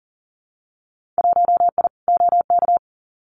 Morse code, heard as a keyed tone, about 40 wpm, 700 Hz. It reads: 1I OK
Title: Winter Field Day Exchanges Encore 40wpm